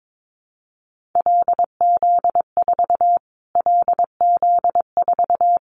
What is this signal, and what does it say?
Morse code, heard as a keyed tone, about 22 wpm, 700 Hz. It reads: LZ4 LZ4
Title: Callsign Prefixes for Top 25 DXCC - Repeated 3x - 22wpm